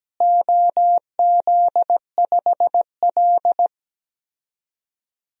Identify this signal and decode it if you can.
Morse code, heard as a keyed tone, about 17 wpm, 700 Hz. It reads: OZ5L
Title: DXCC-100 Call Signs - 4 Characters - Repeated 3x - 17wpm